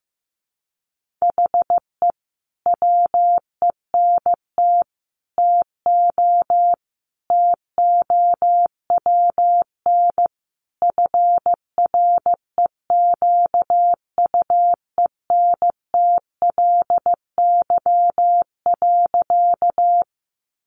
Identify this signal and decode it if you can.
Morse code, heard as a keyed tone, about 15 wpm, 700 Hz. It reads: HE WENT TO TOWN FREQUENTLY.
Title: Sentences from Top 1300 Words 15wpm